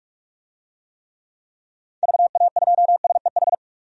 Morse code, heard as a keyed tone, about 42 wpm, 700 Hz. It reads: VA2SEH
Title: DXCC-100 Call Signs - 6 Characters - 45wpm